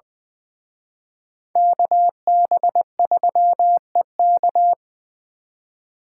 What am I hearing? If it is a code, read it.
Morse code, heard as a keyed tone, about 24 wpm, 700 Hz. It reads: KB3EK
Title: Speed Racing - USA 2x2 Call Signs 20wpm